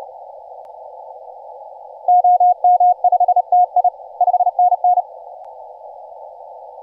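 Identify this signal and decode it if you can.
Morse code, heard as a keyed tone, about 33 wpm, 720 Hz. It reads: OM5TI 5NN
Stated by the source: Morse from DX station 5T5OK on 20 m CW, posted by IZ8GNR.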